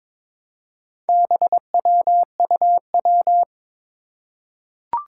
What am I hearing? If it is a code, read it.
Morse code, heard as a keyed tone, about 22 wpm, 700 Hz. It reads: BWUW
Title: Four Challenging Letter-Number Combinations 22wpm